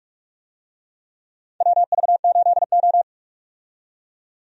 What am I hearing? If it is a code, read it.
Morse code, heard as a keyed tone, about 42 wpm, 700 Hz. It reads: WV8O
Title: USA Call Signs - 4 Characters - 45wpm